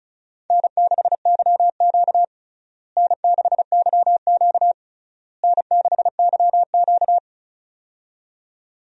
Morse code, heard as a keyed tone, about 34 wpm, 700 Hz. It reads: N6YQ N6YQ N6YQ
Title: USA Call Signs - 4 Characters - Repeated 3x - 35wpm